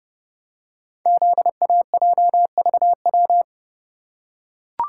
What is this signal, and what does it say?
Morse code, heard as a keyed tone, about 29 wpm, 700 Hz. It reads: ZAJVW
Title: Code Groups - 5 Letters - 30wpm